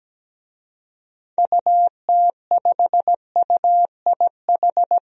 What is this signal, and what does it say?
Morse code, heard as a keyed tone, about 17 wpm, 700 Hz. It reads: UT5UIH
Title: DXCC-100 Call Signs - 6 Characters - Repeated 2x - 17wpm